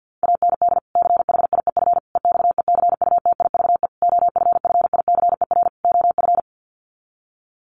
Morse code, heard as a keyed tone, about 50 wpm, 700 Hz. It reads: AND CHIEF EXECUTIVE OFFICER OF
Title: Sets of 5 Words Encore2 50wpm